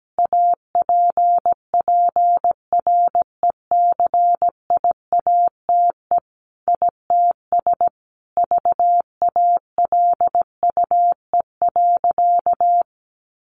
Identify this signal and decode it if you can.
Morse code, heard as a keyed tone, about 17 wpm, 700 Hz. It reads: APPRECIATE ITS VALUE.
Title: Sentences from Top 1300 Words 17wpm